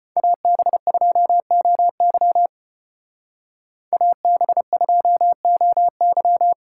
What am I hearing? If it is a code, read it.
Morse code, heard as a keyed tone, about 34 wpm, 700 Hz. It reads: AB2OY AB2OY